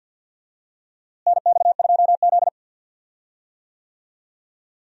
Morse code, heard as a keyed tone, about 49 wpm, 700 Hz. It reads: NX2Z
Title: USA Call Signs - 4 Characters - Repeated 3x - 50wpm